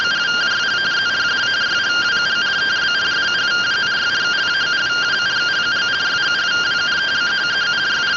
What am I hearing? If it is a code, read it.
Two-tone teleprinter signal, about 170 Hz shift.